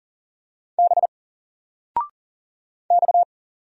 Morse code, heard as a keyed tone, about 39 wpm, 700 Hz. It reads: B X
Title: Single Character (Letter:Number:Punctuation:Prosign) 40wpm